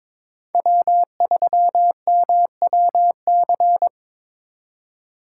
Morse code, heard as a keyed tone, about 26 wpm, 700 Hz. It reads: W3MWC
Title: Speed Racing - USA 1x3 Call Signs 22wpm